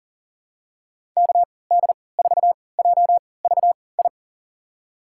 Morse code, heard as a keyed tone, about 40 wpm, spaced as for 23 wpm, 700 Hz. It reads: KD4JVI
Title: USA Call Signs - 3x Character Spacing - 40wpm